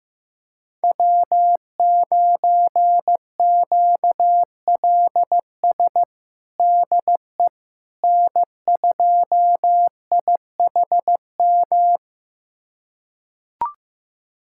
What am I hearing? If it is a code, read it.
Morse code, heard as a keyed tone, about 15 wpm, 700 Hz. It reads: W9QLS DE N2IHM
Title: USA Callsign DE Callsign - 5 Characters - 15wpm